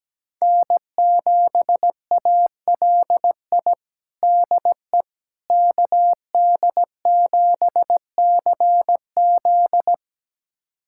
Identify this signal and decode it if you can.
Morse code, heard as a keyed tone, about 17 wpm, 700 Hz. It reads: N7ALI DE KD7CZ
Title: USA Callsign DE Callsign - 5 Characters - 17wpm